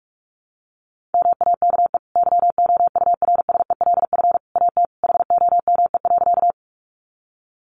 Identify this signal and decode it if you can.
Morse code, heard as a keyed tone, about 44 wpm, 700 Hz. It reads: MAKE YOURSELF AT HOME.
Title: Sentences from Top 600 Words 45wpm